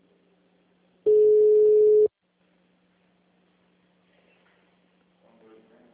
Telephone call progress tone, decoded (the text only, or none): ringback tone